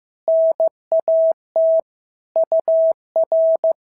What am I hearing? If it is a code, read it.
Morse code, heard as a keyed tone, about 15 wpm, 650 Hz. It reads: NAT UR
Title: SOTA QSO Practice 15wpm